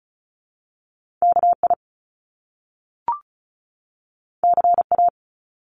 Morse code, heard as a keyed tone, about 36 wpm, 700 Hz. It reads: KI CA
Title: Top Two Letter Combinations in Words 35wpm